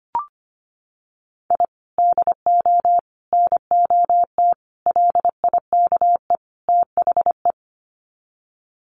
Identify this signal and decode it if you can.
Morse code, heard as a keyed tone, about 25 wpm, 700 Hz. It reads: I DO NOT LIKE THE
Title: Sets of 5 Words Encore 25wpm